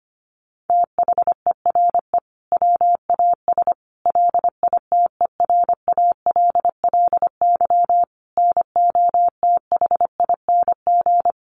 Morse code, heard as a keyed tone, about 25 wpm, 700 Hz. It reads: THERE WAS LITERALLY NOTHING
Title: Sentences from Top 1400 Words 25wpm